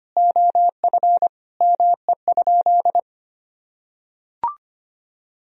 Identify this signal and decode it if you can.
Morse code, heard as a keyed tone, about 25 wpm, 700 Hz. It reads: OF ME?
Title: Sentences from Top 1000 Words - Review Part 4 - 25wpm